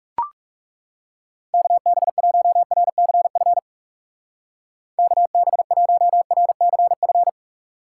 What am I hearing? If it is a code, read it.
Morse code, heard as a keyed tone, about 38 wpm, 700 Hz. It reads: KB1RCF KB1RCF